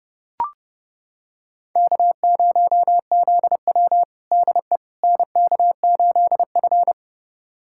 Morse code, heard as a keyed tone, about 30 wpm, 700 Hz. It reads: K0ZW DE NK8F
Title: USA Callsign DE Callsign - 4 Characters - 30wpm